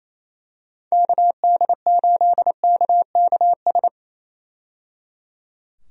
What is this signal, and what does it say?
Morse code, heard as a keyed tone, about 28 wpm, 700 Hz. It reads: KD8KKS
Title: USA Call Signs - 6 Characters - 28wpm